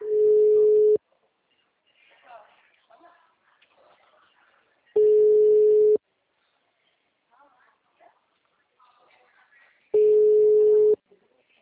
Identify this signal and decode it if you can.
Ringback tone (Europe, 425 Hz), with speech.